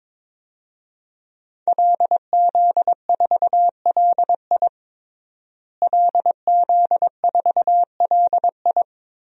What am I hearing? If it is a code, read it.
Morse code, heard as a keyed tone, about 22 wpm, 700 Hz. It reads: LZ4LI LZ4LI
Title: DXCC-100 Call Signs - 5 Characters - Repeated 3x - 22wpm